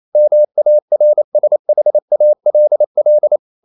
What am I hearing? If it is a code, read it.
Morse code, heard as a keyed tone, about 28 wpm, 600 Hz. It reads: MARSHALL
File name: CW / Practice / NationalCapitals28